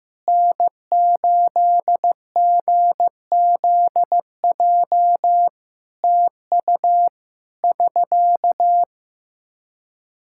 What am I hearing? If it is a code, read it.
Morse code, heard as a keyed tone, about 15 wpm, 700 Hz. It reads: N8GZJ TU <SK>
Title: Practice QSO - Exchange 4 - Ending - 15wpm